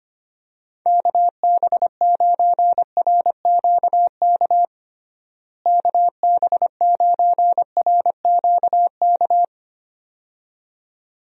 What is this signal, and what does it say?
Morse code, heard as a keyed tone, about 25 wpm, 700 Hz. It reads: KB9RQK KB9RQK
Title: USA Call Signs - 6 Characters - Repeated 2x - 25wpm